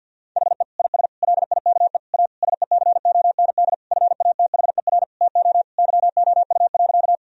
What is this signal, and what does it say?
Morse code, heard as a keyed tone, about 52 wpm, 700 Hz. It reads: HE IS LIKE A SECOND FATHER TO YOU.
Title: Sentences from Top 1000 Words - Review Part 1 - 50wpm